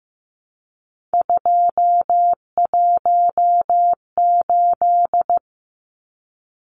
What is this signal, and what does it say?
Morse code, heard as a keyed tone, about 15 wpm, 700 Hz. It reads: 218
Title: Numbers - 3-Digit 15wpm